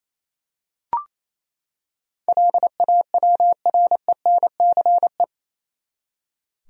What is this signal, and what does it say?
Morse code, heard as a keyed tone, about 28 wpm, 700 Hz. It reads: LAWRENCE
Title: USA Common Names - 28wpm